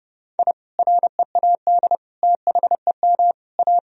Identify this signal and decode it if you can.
Morse code, heard as a keyed tone, about 30 wpm, 700 Hz. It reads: I READ THEM A
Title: Sentences from Top 1000 Words - Review Part 1 - 30wpm